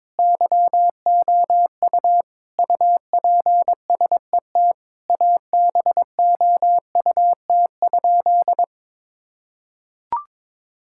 Morse code, heard as a keyed tone, about 22 wpm, 700 Hz. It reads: YOU UPSET ABOUT?
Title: Sentences from Top 1000 Words - Review Part 5 - 22wpm